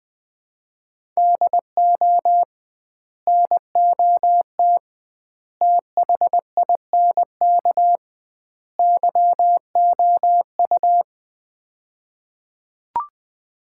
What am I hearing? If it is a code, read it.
Morse code, heard as a keyed tone, about 21 wpm, 700 Hz. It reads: DO NOT THINK YOU
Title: 2x Word-Spacing - Sets of 4 Words - 20wpm